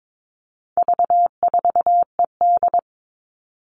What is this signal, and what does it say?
Morse code, heard as a keyed tone, about 22 wpm, 700 Hz. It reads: V4ED
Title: All Cut Numbers - 4-Digit 22wpm